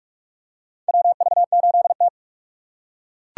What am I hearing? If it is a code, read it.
Morse code, heard as a keyed tone, about 42 wpm, 700 Hz. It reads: WV8T
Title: USA Call Signs - 3x Character Spacing - 45wpm